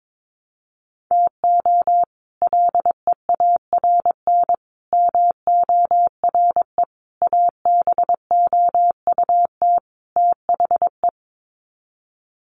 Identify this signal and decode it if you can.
Morse code, heard as a keyed tone, about 22 wpm, 700 Hz. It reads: TO LEARN MORE ABOUT THE
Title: Sets of 5 Words Encore2 22wpm